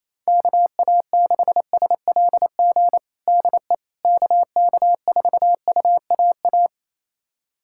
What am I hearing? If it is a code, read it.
Morse code, heard as a keyed tone, about 28 wpm, 700 Hz. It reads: KA6SLG DE KK4UAA
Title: USA Callsign DE Callsign - 6 Characters - 28wpm